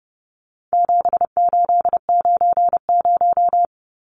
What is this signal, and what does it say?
Morse code, heard as a keyed tone, about 30 wpm, 700 Hz. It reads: 7890
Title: Numbers - 4-Digit 30wpm